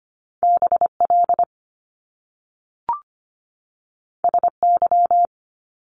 Morse code, heard as a keyed tone, about 25 wpm, 700 Hz. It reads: B L S Y
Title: Top Two Letter Combinations in Words 25wpm